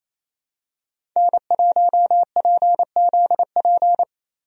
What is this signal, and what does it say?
Morse code, heard as a keyed tone, about 28 wpm, 700 Hz. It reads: N1PZP
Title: USA Call Signs - 5 Characters - 28wpm